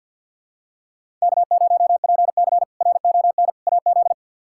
Morse code, heard as a keyed tone, about 52 wpm, 700 Hz. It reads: K0PC RON AZ
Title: K1USN SST QSO Practice - 50wpm